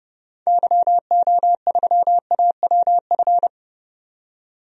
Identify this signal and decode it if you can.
Morse code, heard as a keyed tone, about 30 wpm, 700 Hz. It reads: YO3AWF
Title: DXCC-100 Call Signs - 6 Characters - 30wpm